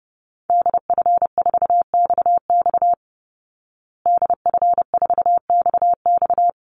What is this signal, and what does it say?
Morse code, heard as a keyed tone, about 28 wpm, 700 Hz. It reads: DF4XX DF4XX
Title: Call Signs Repeated 3x - 30wpm